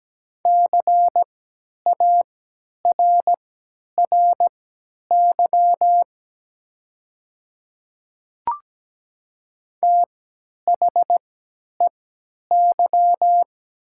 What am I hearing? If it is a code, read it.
Morse code, heard as a keyed tone, about 17 wpm, spaced as for 10 wpm, 700 Hz. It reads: CARRY THEY
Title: 3x Character Spacing - Top 300 Words - 17wpm